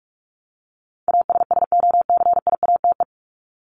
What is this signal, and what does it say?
Morse code, heard as a keyed tone, about 44 wpm, 700 Hz. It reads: ASSOCIATE
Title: ED Suffix - Top 100 Words with Base Words 45wpm